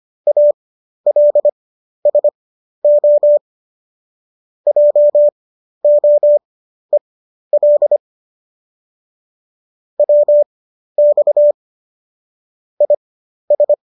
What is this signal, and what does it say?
Morse code, heard as a keyed tone, about 25 wpm, spaced as for 12 wpm, 590 Hz. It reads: ALSO JOEL WX IS
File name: qso009 12